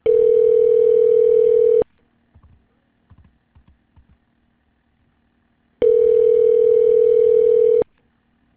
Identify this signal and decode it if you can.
Ringback tone (North America), 440 + 480 Hz.